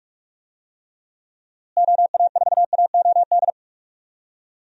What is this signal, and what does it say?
Morse code, heard as a keyed tone, about 43 wpm, 700 Hz. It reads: OA4AOD
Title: USA Call Signs - 6 Characters - Repeated 2x - 45wpm